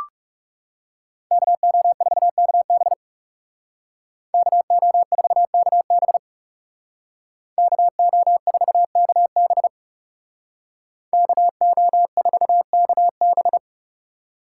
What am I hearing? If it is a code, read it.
Morse code, heard as a keyed tone, about 37 wpm, 700 Hz. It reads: KO4KB KO4KB KO4KB KO4KB
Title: Speed Racing - USA 2x2 Call Signs 30wpm